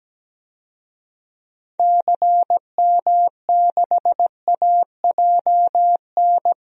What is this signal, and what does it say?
Morse code, heard as a keyed tone, about 17 wpm, 700 Hz. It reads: CM6AJN
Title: DXCC-100 Call Signs - 6 Characters - Repeated 3x - 17wpm